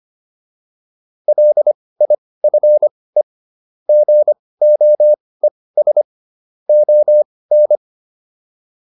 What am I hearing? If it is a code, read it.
Morse code, heard as a keyed tone, about 25 wpm, spaced as for 18 wpm, 600 Hz. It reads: LIFE GOES ON
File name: PR305 18